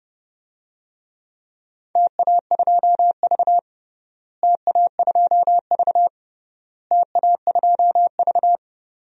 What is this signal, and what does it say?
Morse code, heard as a keyed tone, about 30 wpm, 700 Hz. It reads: TA2V TA2V TA2V
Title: DXCC-100 Call Signs - 4 Characters - Repeated 3x - 30wpm